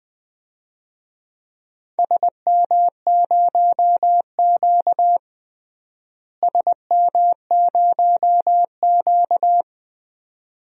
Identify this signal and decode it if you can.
Morse code, heard as a keyed tone, about 19 wpm, 700 Hz. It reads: SM0Q SM0Q
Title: DXCC-100 Call Signs - 4 Characters - Repeated 2x - 20wpm